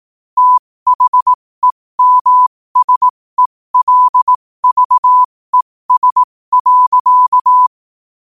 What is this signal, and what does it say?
Morse code, heard as a keyed tone, about 18 wpm, spaced as for 15 wpm, 1000 Hz. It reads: THEMSELVES.
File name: qotdfiles / 15wpm